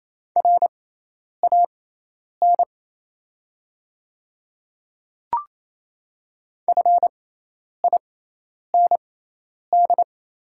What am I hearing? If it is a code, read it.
Morse code, heard as a keyed tone, about 28 wpm, spaced as for 10 wpm, 700 Hz. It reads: RAN FIND
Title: 6x Character Spacing - Top 500 Words - 28wpm